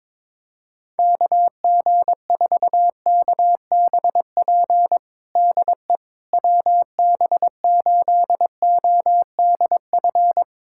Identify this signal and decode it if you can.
Morse code, heard as a keyed tone, about 22 wpm, 700 Hz. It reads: KG4KBP DE WB8ODF
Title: USA Callsign DE Callsign 22wpm